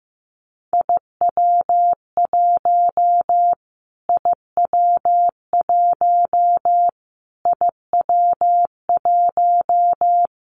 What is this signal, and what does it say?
Morse code, heard as a keyed tone, about 15 wpm, 700 Hz. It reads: IW1 IW1 IW1
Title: Callsign Prefixes for Top 25 DXCC - Repeated 3x - 15wpm